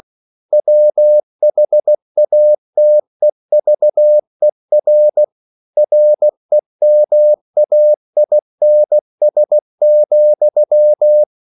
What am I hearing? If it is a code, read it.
Morse code, heard as a keyed tone, about 16 wpm, 600 Hz. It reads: WHATEVER REMAINS,
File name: CW / Practice / EnglishQuotations0216